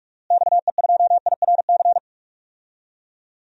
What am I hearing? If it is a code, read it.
Morse code, heard as a keyed tone, about 46 wpm, 700 Hz. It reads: XE2IRC
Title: DXCC-100 Call Signs - 6 Characters - Repeated 3x - 45wpm